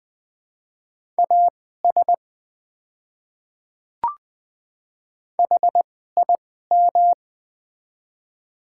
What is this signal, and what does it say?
Morse code, heard as a keyed tone, about 20 wpm, spaced as for 14 wpm, 700 Hz. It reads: AS HIM
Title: 2x Character Spacing - Top 100 Words - 20wpm